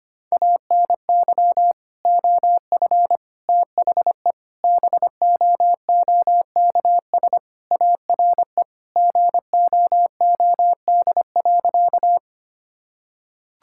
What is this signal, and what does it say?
Morse code, heard as a keyed tone, about 25 wpm, 700 Hz. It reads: ANY OF THE BOOKS ARE GOOD.
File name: Sentences with 3-Word Phrases - 25wpm